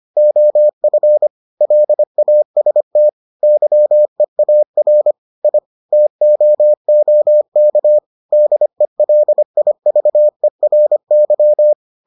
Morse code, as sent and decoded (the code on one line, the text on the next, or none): --- ..-. / .-.. .- ... - / -.-- . .- .-. / .. / - --- --- -.- / -.. .-.. .. ...- . .-. -.--
OF LAST YEAR I TOOK DELIVERY